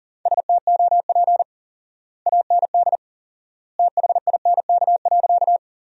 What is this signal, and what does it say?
Morse code, heard as a keyed tone, about 40 wpm, 700 Hz. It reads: STOP AND THINK.
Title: 4x Word-Spacing - Sentences from Top 300 Words - 40wpm